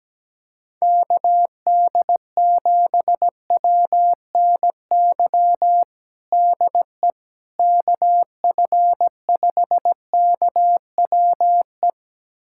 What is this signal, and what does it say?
Morse code, heard as a keyed tone, about 17 wpm, 700 Hz. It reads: KD7WNY DE KF5KWE
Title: USA Callsign DE Callsign - 6 Characters - 17wpm